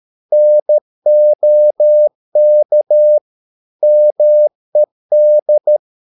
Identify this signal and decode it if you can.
Morse code, heard as a keyed tone, about 13 wpm, 600 Hz. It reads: NOK MED
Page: Ferie 13 wpm